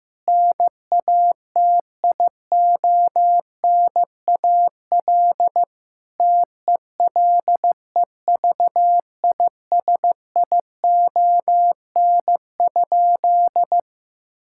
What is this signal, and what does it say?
Morse code, heard as a keyed tone, about 15 wpm, 700 Hz. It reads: NATIONAL TELEVISION?
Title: Sentences from Top 1000 Words - Review Part 5 - 15wpm